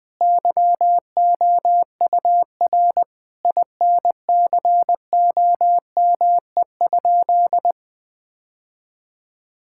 Morse code, heard as a keyed tone, about 20 wpm, 700 Hz. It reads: YOUR INCOME?
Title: Sentences from Top 1000 Words - Review Part 5 - 20wpm